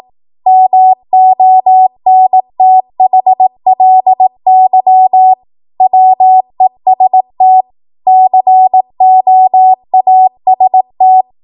Morse code, heard as a keyed tone, about 18 wpm, 750 Hz. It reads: MONTHLY WEST COAST
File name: Archive / 18 WPM / 18WPM